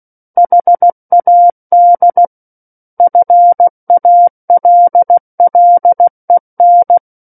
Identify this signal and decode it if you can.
Morse code, heard as a keyed tone, about 16 wpm, 700 Hz. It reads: HAD FALLEN
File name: Book One / 07-Chapter 7 - 16 WPM